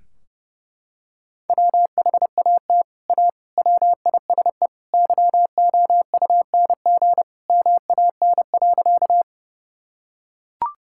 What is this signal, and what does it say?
Morse code, heard as a keyed tone, about 30 wpm, 700 Hz. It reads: WHAT A WISE YOUNG MAN.
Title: Sentences from Top 1600 Words 30wpm